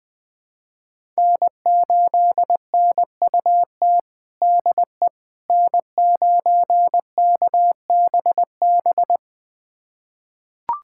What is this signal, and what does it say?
Morse code, heard as a keyed tone, about 21 wpm, 700 Hz. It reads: N8NUT DE N9KBB
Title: USA Callsign DE Callsign - 5 Characters - 20wpm